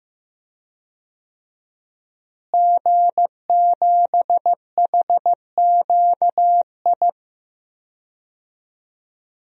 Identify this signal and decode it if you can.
Morse code, heard as a keyed tone, about 15 wpm, 700 Hz. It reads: G7HQI
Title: DXCC-100 Call Signs - 5 Characters - Repeated 2x - 15wpm